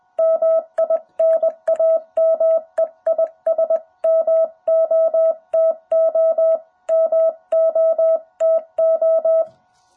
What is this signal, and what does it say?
Morse code, heard as a keyed tone, about 20 wpm, 660 Hz. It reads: MI NAME IS MOTO MOTO
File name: jh8cbh / 2021cw